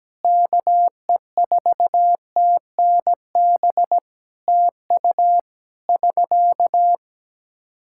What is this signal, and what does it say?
Morse code, heard as a keyed tone, about 17 wpm, 700 Hz. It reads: KE4TNB TU <SK>
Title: Practice QSO - Exchange 4 - Ending - 17wpm